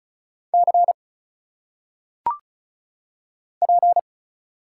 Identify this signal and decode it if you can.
Morse code, heard as a keyed tone, about 36 wpm, 700 Hz. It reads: C P
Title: Single Letters - Phonetic Pronunciation 35wpm